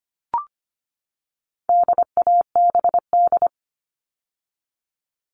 Morse code, heard as a keyed tone, about 25 wpm, 700 Hz. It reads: DABD
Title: All Cut Numbers - 4-Digit 25wpm